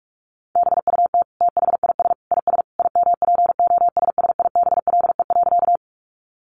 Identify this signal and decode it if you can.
Morse code, heard as a keyed tone, about 44 wpm, 700 Hz. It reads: BUT THIS IS IMPOSSIBLE.
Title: Sentences from Top 1100 Words 45wpm